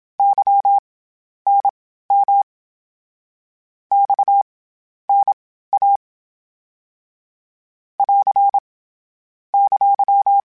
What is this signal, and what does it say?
Morse code, heard as a keyed tone, about 27 wpm, spaced as for 12 wpm, 800 Hz. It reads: YNM XNA <AR> !
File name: CW Comparisons / CW Comparisons-mp3s / CW Comparisons Broken Characters @25